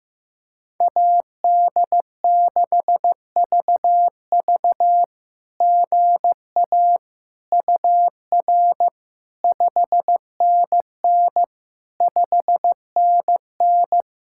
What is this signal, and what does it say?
Morse code, heard as a keyed tone, about 16 wpm, 700 Hz. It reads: AD6VV GA UR 5NN 5NN
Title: SOTA QSO Practice 15wpm